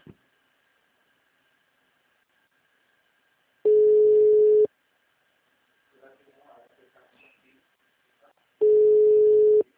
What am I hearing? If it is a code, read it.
Ringback tone (Europe, 425 Hz), with speech.